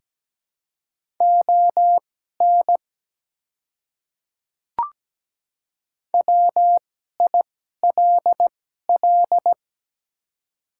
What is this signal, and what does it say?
Morse code, heard as a keyed tone, about 17 wpm, spaced as for 12 wpm, 700 Hz. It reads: ON WILL
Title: 2x Character Spacing - Top 200 Words - 17wpm